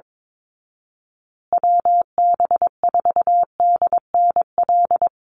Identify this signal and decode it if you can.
Morse code, heard as a keyed tone, about 22 wpm, 690 Hz. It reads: WB4DNL
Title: Call Signs Repeated 2x - 22wpm